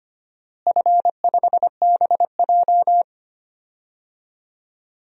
Morse code, heard as a keyed tone, about 25 wpm, 700 Hz. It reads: F5BJ
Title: DXCC-100 Call Signs - 4 Characters - Repeated 2x - 25wpm